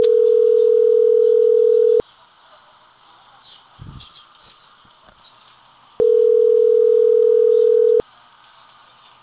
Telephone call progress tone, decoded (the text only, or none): ringback tone